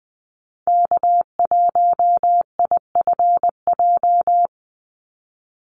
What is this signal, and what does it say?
Morse code, heard as a keyed tone, about 20 wpm, 700 Hz. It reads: K1IFJ
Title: Call Signs Repeated 3x - 20wpm